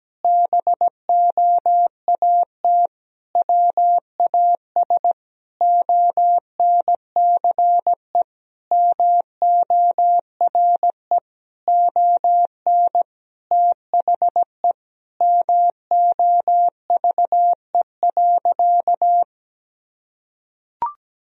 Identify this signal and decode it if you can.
Morse code, heard as a keyed tone, about 17 wpm, 700 Hz. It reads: BOAT WAS ONCE MORE ON THE MOVE.
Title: Sentences from Top 1000 Words - Review Part 3 - 17wpm